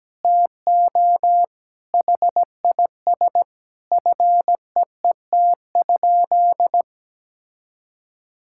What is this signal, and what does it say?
Morse code, heard as a keyed tone, about 17 wpm, 700 Hz. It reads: TO HIS FEET?
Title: Sentences from Top 1000 Words - Review Part 6 - 17wpm